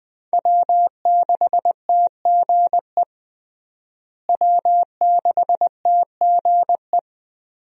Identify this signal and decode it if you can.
Morse code, heard as a keyed tone, about 20 wpm, 700 Hz. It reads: W6TGE W6TGE
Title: USA Call Signs - 5 Characters - Repeated 3x - 20wpm